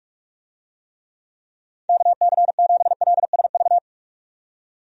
Morse code, heard as a keyed tone, about 47 wpm, 700 Hz. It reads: KC7LSV
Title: USA Call Signs - 6 Characters - 45wpm